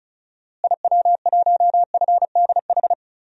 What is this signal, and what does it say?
Morse code, heard as a keyed tone, about 36 wpm, 700 Hz. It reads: IW1FDH
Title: DXCC-100 Call Signs - 6 Characters - Repeated 3x - 35wpm